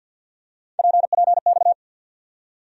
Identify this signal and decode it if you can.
Morse code, heard as a keyed tone, about 48 wpm, 700 Hz. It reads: PPX
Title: USA Call Sign Postfixes - 3 Letters 50wpm